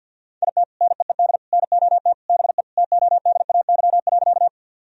Morse code, heard as a keyed tone, about 51 wpm, 700 Hz. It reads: IT NEED NOT BE TODAY.
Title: Sentences from Top 1000 Words - Review Part 4 - 50wpm